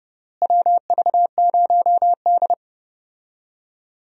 Morse code, heard as a keyed tone, about 30 wpm, 700 Hz. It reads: WV0D
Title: USA Call Signs - 4 Characters - Repeated 3x - 30wpm